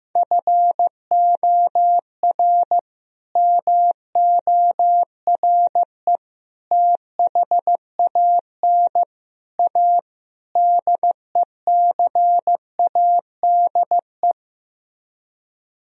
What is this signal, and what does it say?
Morse code, heard as a keyed tone, about 15 wpm, 700 Hz. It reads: FOR MORE THAN A DECADE
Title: Speed-Racing - Sets of 5 Words 15wpm